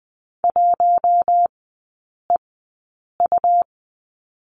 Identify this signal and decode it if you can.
Morse code, heard as a keyed tone, about 20 wpm, spaced as for 8 wpm, 700 Hz. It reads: 1EU
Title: 107 Lesson 18e - Letter m - Three Characters 20wpm